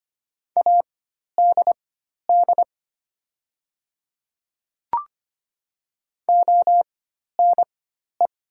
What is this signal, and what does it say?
Morse code, heard as a keyed tone, about 25 wpm, spaced as for 12 wpm, 700 Hz. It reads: ADD ONE
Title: 4x Character Spacing - Top 500 Words - 25wpm